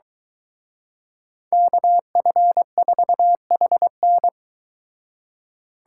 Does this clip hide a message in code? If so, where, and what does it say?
Yes: Morse code from 1.5 to 4.3 s, heard as a keyed tone, about 23 wpm, 700 Hz. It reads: KF4HN